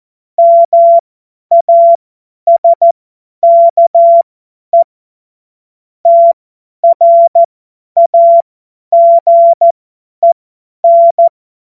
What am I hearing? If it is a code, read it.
Morse code, heard as a keyed tone, about 14 wpm, spaced as for 10 wpm, 680 Hz. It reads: MASKE TRAGEN
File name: Morsekurs / Witze / Witz 31. August 2021, 50 F70